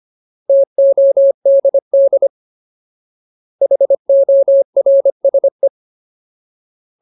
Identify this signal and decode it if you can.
Morse code, heard as a keyed tone, about 25 wpm, 550 Hz. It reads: TODD HORSE
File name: Ultimate Word List / 25WPM Ultimate